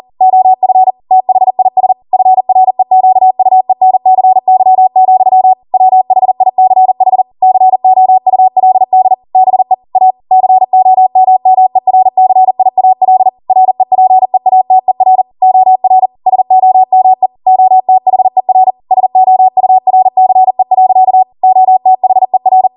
Morse code, heard as a keyed tone, about 40 wpm, 750 Hz. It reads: OF THIS FREQUENCY, WHICH COULD BE A COMMERCIAL REPEATER OR SOME OTHER SOURCE. OTHER
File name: Morse / 40WPM